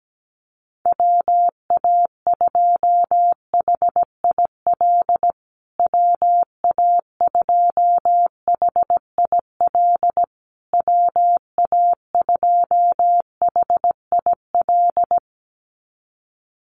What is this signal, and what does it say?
Morse code, heard as a keyed tone, about 17 wpm, 700 Hz. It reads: WA2HIL WA2HIL WA2HIL
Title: USA Call Signs Repeated 3x 17wpm